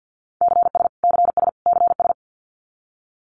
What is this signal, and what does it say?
Morse code, heard as a keyed tone, about 50 wpm, 700 Hz. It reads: CS CS CS
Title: Abbreviations Repeated 3x - 50wpm